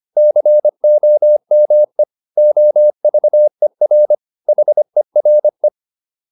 Morse code, heard as a keyed tone, about 25 wpm, 600 Hz. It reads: COME OVER HERE
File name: PR302 25